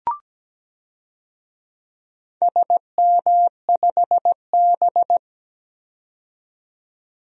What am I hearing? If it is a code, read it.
Morse code, heard as a keyed tone, about 17 wpm, 700 Hz. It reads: SM5B
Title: DXCC-100 Call Signs - 4 Characters - 17wpm